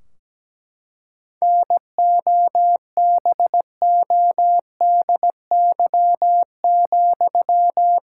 Morse code, heard as a keyed tone, about 17 wpm, 700 Hz. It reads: NOBODY,
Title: Sentences from Top 1600 Words 17wpm